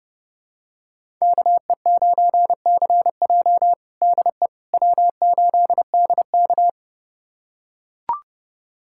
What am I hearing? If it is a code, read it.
Morse code, heard as a keyed tone, about 30 wpm, 700 Hz. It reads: KE9CJ DE W8DK
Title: USA Callsign DE Callsign 30wpm